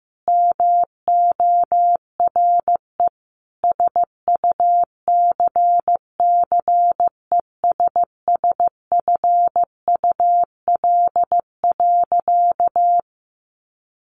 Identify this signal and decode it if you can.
Morse code, heard as a keyed tone, about 15 wpm, 700 Hz. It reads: MORE SUCCESSFUL.
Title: Sentences from Top 1500 Words 15wpm